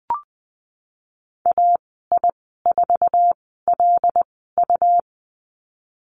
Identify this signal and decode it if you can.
Morse code, heard as a keyed tone, about 20 wpm, spaced as for 15 wpm, 700 Hz. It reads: AI4LU
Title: 186 Lesson 30g - Letter b - Call Signs 20wpm